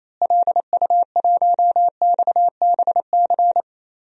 Morse code, heard as a keyed tone, about 27 wpm, 700 Hz. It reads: LU1XBC
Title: DXCC-100 Call Signs - 6 Characters - Repeated 3x - 28wpm